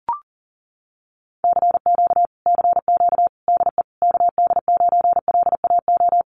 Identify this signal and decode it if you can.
Morse code, heard as a keyed tone, about 40 wpm, 700 Hz. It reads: CQ CQ DE KD9LAO
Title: Common QSO Phrases Encore 40wpm